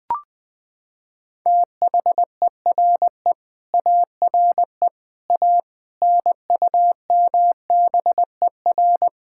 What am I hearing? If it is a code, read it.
Morse code, heard as a keyed tone, about 20 wpm, 700 Hz. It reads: THERE ARE A NUMBER
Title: Sets of 8 Words - 20wpm